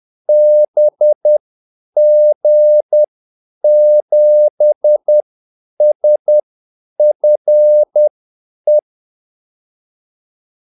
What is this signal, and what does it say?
Morse code, heard as a keyed tone, about 10 wpm, spaced as for 8 wpm, 600 Hz. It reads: BG7SFE
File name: Callsigns-10-8wpm